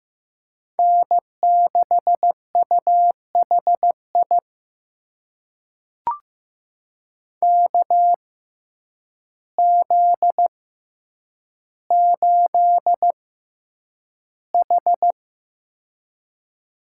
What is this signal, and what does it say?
Morse code, heard as a keyed tone, about 15 wpm, spaced as for 8 wpm, 700 Hz. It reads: N6UHI KZ8H
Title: USA Call Signs - 6x Character Spacing - 15wpm